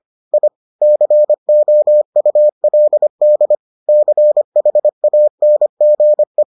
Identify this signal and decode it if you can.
Morse code, heard as a keyed tone, about 25 wpm, 600 Hz. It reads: I COULD CHANGE